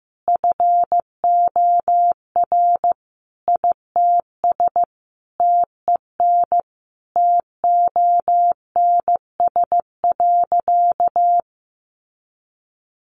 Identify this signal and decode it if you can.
Morse code, heard as a keyed tone, about 15 wpm, 700 Hz. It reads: FOR ITS TEN TONS.
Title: A Sound Of Thunder 15wpm